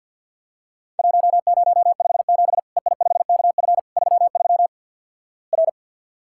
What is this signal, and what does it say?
Morse code, heard as a keyed tone, about 50 wpm, 700 Hz. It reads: 1057 EI5KF 33 R
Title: Worked All Europe DX Contest - QTC Exchange - 50wpm